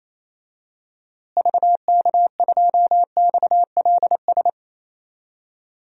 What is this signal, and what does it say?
Morse code, heard as a keyed tone, about 28 wpm, 700 Hz. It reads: VK2XLS
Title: DXCC-100 Call Signs - 6 Characters - Repeated 3x - 28wpm